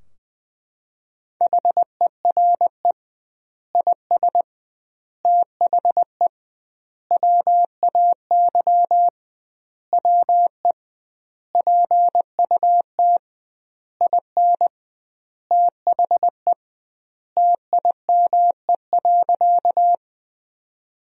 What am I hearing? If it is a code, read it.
Morse code, heard as a keyed tone, about 20 wpm, 700 Hz. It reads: HERE IS THE WAY WE PUT IN THE TIME.
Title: 2x Word-Spacing - Sentences from Top 200 Words - 20wpm